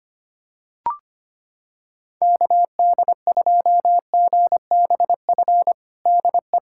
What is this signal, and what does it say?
Morse code, heard as a keyed tone, about 25 wpm, 700 Hz. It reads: KD2GBF DE
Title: USA Callsign DE Callsign 25wpm